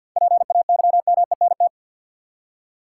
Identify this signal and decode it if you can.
Morse code, heard as a keyed tone, about 52 wpm, 700 Hz. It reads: PAYMENT
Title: Seven Letter Words - 50wpm